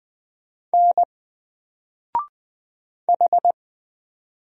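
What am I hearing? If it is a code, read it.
Morse code, heard as a keyed tone, about 20 wpm, 700 Hz. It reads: N H
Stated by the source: Single letter and number Morse drill at 20 wpm, from N4LCD.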